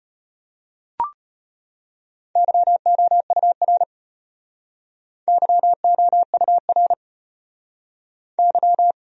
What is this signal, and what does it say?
Morse code, heard as a keyed tone, about 36 wpm, 700 Hz. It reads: YOUR YOUR Y